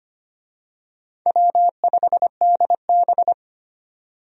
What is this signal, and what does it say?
Morse code, heard as a keyed tone, about 25 wpm, 700 Hz. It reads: W5DB
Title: USA Call Signs - 4 Characters - 25wpm